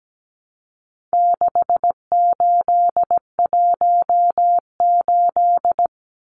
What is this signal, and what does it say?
Morse code, heard as a keyed tone, about 17 wpm, 700 Hz. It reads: 6818
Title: Numbers - 4-Digit 17wpm